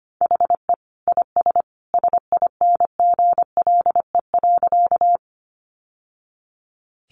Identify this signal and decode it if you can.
Morse code, heard as a keyed tone, about 25 wpm, 700 Hz. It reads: HE IS SINGLE.
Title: Sentences from Top 700 Words 25wpm